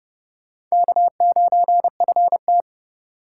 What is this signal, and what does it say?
Morse code, heard as a keyed tone, about 37 wpm, 700 Hz. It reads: K9FT
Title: Speed Racing - USA Call Signs 30wpm